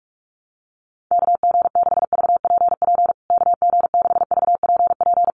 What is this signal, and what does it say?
Morse code, heard as a keyed tone, about 44 wpm, 690 Hz. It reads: KG6VPP KG6VPP
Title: USA Call Signs Repeated 3x 45wpm